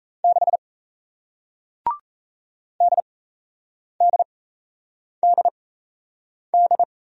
Morse code, heard as a keyed tone, about 37 wpm, spaced as for 8 wpm, 700 Hz. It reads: 6 DDDD